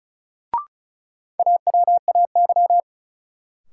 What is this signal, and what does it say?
Morse code, heard as a keyed tone, about 34 wpm, 700 Hz. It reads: AWAY
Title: Sets of 2 Words - New Word Drill - 35wpm